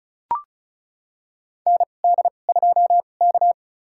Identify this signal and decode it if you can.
Morse code, heard as a keyed tone, about 36 wpm, spaced as for 25 wpm, 700 Hz. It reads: ND2K
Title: USA Call Signs - 4 Characters - 2x Character Spacing - 35wpm